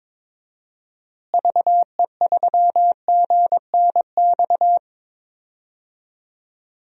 Morse code, heard as a keyed tone, about 22 wpm, 700 Hz. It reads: VE3GNX
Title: DXCC-100 Call Signs - 6 Characters - Repeated 2x - 22wpm